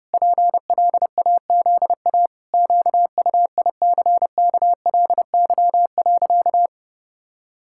Morse code, heard as a keyed tone, about 28 wpm, 700 Hz. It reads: PLAZA QUICKLY.
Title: Pangram Challenge v2 - 30wpm